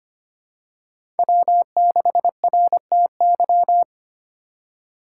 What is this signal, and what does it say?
Morse code, heard as a keyed tone, about 25 wpm, 700 Hz. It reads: W6RTY
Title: USA Call Signs - 2x Character Spacing - 25wpm